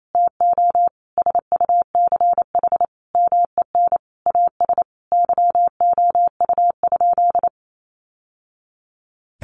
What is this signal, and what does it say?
Morse code, heard as a keyed tone, about 28 wpm, 700 Hz. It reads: TO SUCH MEN AS YOU?
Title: Sentences from Top 1000 Words 28wpm